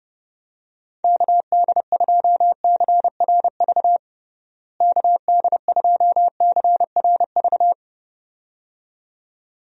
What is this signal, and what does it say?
Morse code, heard as a keyed tone, about 32 wpm, 700 Hz. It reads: KD2CRV KD2CRV
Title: USA Call Signs - 6 Characters - Repeated 2x - 30wpm